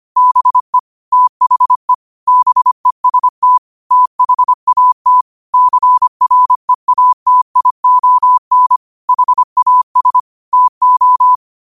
Morse code, heard as a keyed tone, about 25 wpm, 1000 Hz. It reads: BE THE BEST THAT CREATION HAS TO
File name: qotdfiles / 25wpm